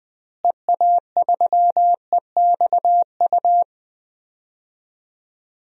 Morse code, heard as a keyed tone, about 20 wpm, 700 Hz. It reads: EA3EXU
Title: DXCC-100 Call Signs - 6 Characters - Repeated 2x - 20wpm